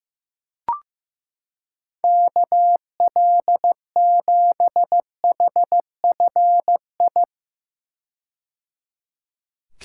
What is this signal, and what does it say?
Morse code, heard as a keyed tone, about 15 wpm, 700 Hz. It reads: KL7HFI
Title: K1USN SST QSO Practice - 15wpm